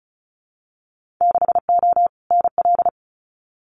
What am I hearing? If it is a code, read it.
Morse code, heard as a keyed tone, about 34 wpm, 700 Hz. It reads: 6O NL
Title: Winter Field Day Exchanges Encore 35wpm